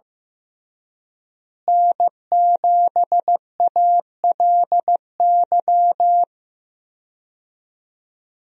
Morse code, heard as a keyed tone, about 15 wpm, 700 Hz. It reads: N7ALY
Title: USA Call Signs - 5 Characters - Repeated 2x - 15wpm